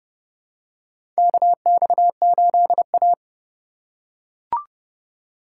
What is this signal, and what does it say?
Morse code, heard as a keyed tone, about 30 wpm, 700 Hz. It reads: KX8A
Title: USA Call Signs - 4 Characters - 2x Character Spacing - 30wpm